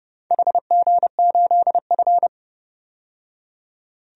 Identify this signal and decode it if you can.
Morse code, heard as a keyed tone, about 30 wpm, 700 Hz. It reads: HG8F
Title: DXCC-100 Call Signs - 4 Characters - 30wpm